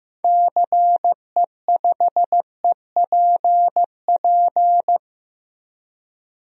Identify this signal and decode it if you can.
Morse code, heard as a keyed tone, about 15 wpm, 700 Hz. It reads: CE5EPP
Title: DXCC-100 Call Signs - 6 Characters - Repeated 3x - 15wpm